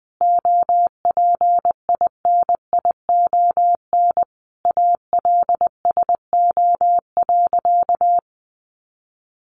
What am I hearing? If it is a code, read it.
Morse code, heard as a keyed tone, about 20 wpm, 700 Hz. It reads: OPINION ALSO.
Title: Sentences from Top 1200 Words 20wpm